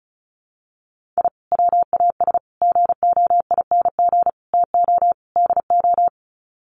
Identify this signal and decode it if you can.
Morse code, heard as a keyed tone, about 34 wpm, 700 Hz. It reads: I WAS GOING TO DO
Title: Sets of 5 Words Encore2 35wpm